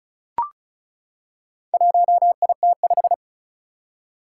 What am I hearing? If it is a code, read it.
Morse code, heard as a keyed tone, about 36 wpm, 690 Hz. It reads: 1IT5
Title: Four Challenging Letter-Number Combinations 35wpm